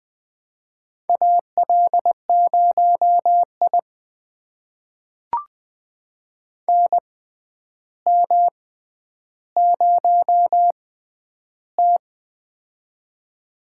Morse code, heard as a keyed tone, about 21 wpm, spaced as for 10 wpm, 700 Hz. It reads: AL0I NM0T
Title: USA Call Signs - 4 Characters - 6x Character Spacing - 20wpm